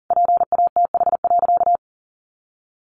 Morse code, heard as a keyed tone, about 40 wpm, 700 Hz. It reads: PATH.
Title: A Sound Of Thunder 40wpm